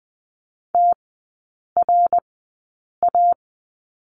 Morse code, heard as a keyed tone, about 20 wpm, spaced as for 8 wpm, 700 Hz. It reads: TRA
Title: 053 Lesson 10e - Letter r - Three Characters 20wpm